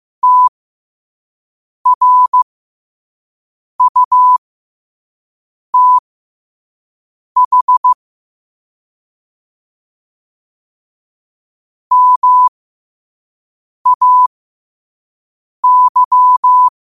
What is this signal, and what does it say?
Morse code, heard as a keyed tone, about 15 wpm, spaced as for 5 wpm, 1000 Hz. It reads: TRUTH MAY
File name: 5wpm